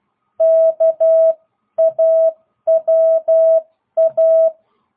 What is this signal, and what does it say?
Morse code, heard as a keyed tone, about 12 wpm, 660 Hz. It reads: KAWA